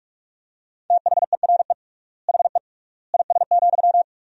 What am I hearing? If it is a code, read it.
Morse code, heard as a keyed tone, about 46 wpm, 690 Hz. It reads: THERE HE IS,
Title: 3x Word-Spacing - Sentences from Top 100 Words - 45wpm